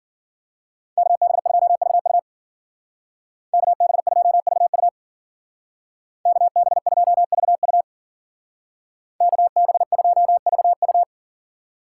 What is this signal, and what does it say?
Morse code, heard as a keyed tone, about 49 wpm, 700 Hz. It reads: KB2VU KB2VU KB2VU KB2VU
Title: Speed Racing - USA 2x2 Call Signs 40wpm